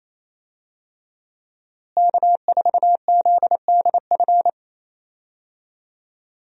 Morse code, heard as a keyed tone, about 28 wpm, 700 Hz. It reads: K4ZDF
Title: USA Call Signs - 5 Characters - Repeated 3x - 28wpm